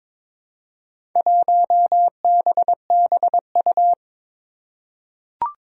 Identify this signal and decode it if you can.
Morse code, heard as a keyed tone, about 23 wpm, 700 Hz. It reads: 1BBU
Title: Four Challenging Letter-Number Combinations 22wpm